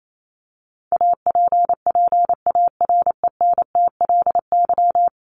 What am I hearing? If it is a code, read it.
Morse code, heard as a keyed tone, about 28 wpm, 700 Hz. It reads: APPARENTLY
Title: LY Suffix - Top 100 Words with Base Words 28wpm